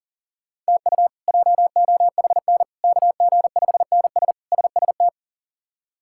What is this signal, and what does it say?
Morse code, heard as a keyed tone, about 40 wpm, 700 Hz. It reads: TU JOHN KG5NS SST
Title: K1USN SST QSO Practice - 40wpm